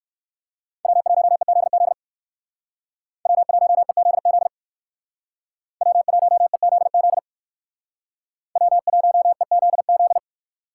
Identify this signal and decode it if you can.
Morse code, heard as a keyed tone, about 53 wpm, 700 Hz. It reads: W1EZZ W1EZZ W1EZZ W1EZZ
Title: Speed Racing - USA 1x3 Call Signs 45wpm